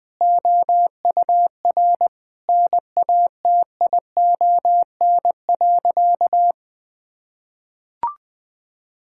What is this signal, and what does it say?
Morse code, heard as a keyed tone, about 20 wpm, 700 Hz. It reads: OUR NATION.
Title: Sentences from Top 1000 Words - Review Part 6 - 20wpm